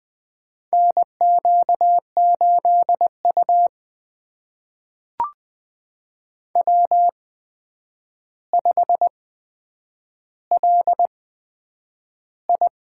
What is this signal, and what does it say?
Morse code, heard as a keyed tone, about 20 wpm, spaced as for 9 wpm, 700 Hz. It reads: NQ8U W5LI
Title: USA Call Signs - 4 Characters - 8x Character Spacing - 20wpm